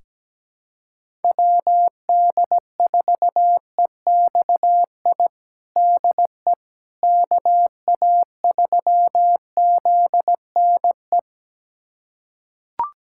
Morse code, heard as a keyed tone, about 17 wpm, 700 Hz. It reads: WD4EXI DE KA3ZNE
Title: USA Callsign DE Callsign - 6 Characters - 17wpm